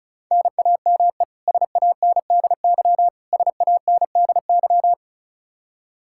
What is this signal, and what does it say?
Morse code, heard as a keyed tone, about 36 wpm, 700 Hz. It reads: NAME SANDY SANDY
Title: Practice QSO - Exchange 1 - Beginning - 35wpm